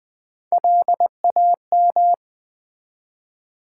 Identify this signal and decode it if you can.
Morse code, heard as a keyed tone, about 20 wpm, 700 Hz. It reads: LAM
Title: Build A Word - 1 Letter At A Time - 4-Letter Words - 20wpm